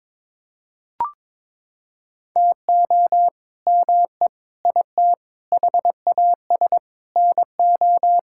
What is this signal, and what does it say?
Morse code, heard as a keyed tone, about 22 wpm, 700 Hz. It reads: TO ME IT HAS NO
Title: Sentences from Top 1000 Words - Review Part 6 - 22wpm